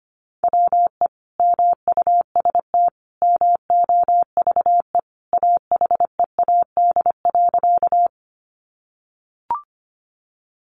Morse code, heard as a keyed tone, about 25 wpm, 700 Hz. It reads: WE MUST MOVE AHEAD.
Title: Sentences from Top 1100 Words 25wpm